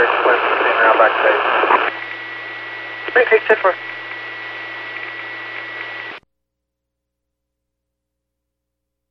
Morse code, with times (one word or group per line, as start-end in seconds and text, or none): none